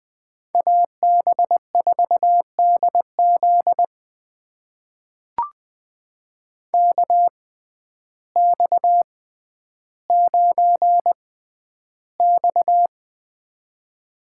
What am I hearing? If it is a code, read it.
Morse code, heard as a keyed tone, about 20 wpm, spaced as for 11 wpm, 700 Hz. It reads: AB4DZ KX9X
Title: USA Call Signs - 6x Character Spacing - 20wpm